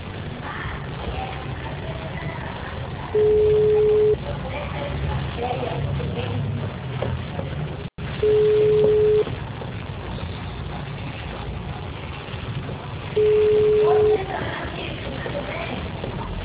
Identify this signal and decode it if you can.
Ringback tone (Europe, 425 Hz), with speech.
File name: out- -2014- - - .11843